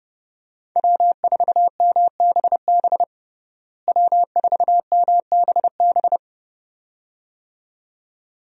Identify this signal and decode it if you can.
Morse code, heard as a keyed tone, about 32 wpm, 700 Hz. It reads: W4MBB W4MBB
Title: USA Call Signs - 5 Characters - Repeated 2x - 30wpm